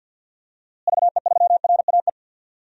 Morse code, heard as a keyed tone, about 47 wpm, 700 Hz. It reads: VE3RAE
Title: DXCC-100 Call Signs - 6 Characters - Repeated 2x - 50wpm